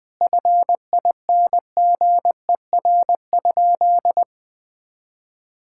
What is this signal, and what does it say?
Morse code, heard as a keyed tone, about 21 wpm, 700 Hz. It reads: FINGER?
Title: Sentences from Top 1600 Words 20wpm